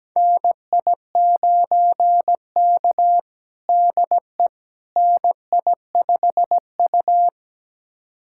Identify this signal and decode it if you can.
Morse code, heard as a keyed tone, about 17 wpm, 700 Hz. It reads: NI9K DE NI5U
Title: USA Callsign DE Callsign - 4 Characters - 17wpm